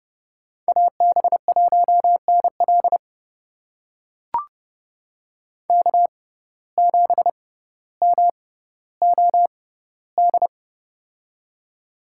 Morse code, heard as a keyed tone, about 30 wpm, spaced as for 15 wpm, 700 Hz. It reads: AB1NL K7MOD
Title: USA Call Signs - 5 Characters - 6x Character Spacing - 30wpm